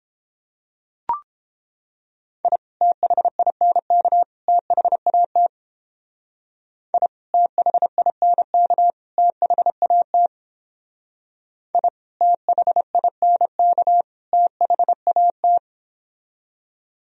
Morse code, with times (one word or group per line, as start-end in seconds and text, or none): 2.4-2.6 I
2.8-4.2 THINK
4.5-5.5 THAT
6.9-7.1 I
7.3-8.9 THINK
9.2-10.3 THAT
11.7-11.9 I
12.2-14.0 THINK
14.3-15.6 THAT